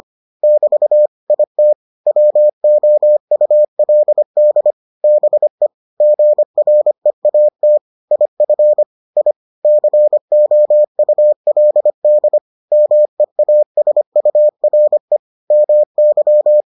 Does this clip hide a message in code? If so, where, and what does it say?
Yes: Morse code from 0.4 to 16.6 s, heard as a keyed tone, about 25 wpm, 600 Hz. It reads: <BT> IT WOULD BE GREAT IF I COULD MEASURE MY